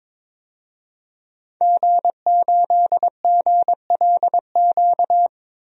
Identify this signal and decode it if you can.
Morse code, heard as a keyed tone, about 22 wpm, 700 Hz. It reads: G8GLQ
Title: DXCC-100 Call Signs - 5 Characters - 22wpm